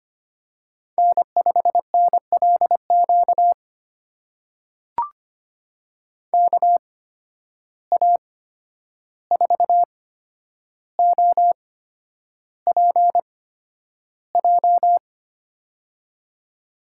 Morse code, heard as a keyed tone, about 25 wpm, spaced as for 10 wpm, 700 Hz. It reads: N5NLQ KA4OPJ
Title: USA Call Signs - 8x Character Spacing - 25wpm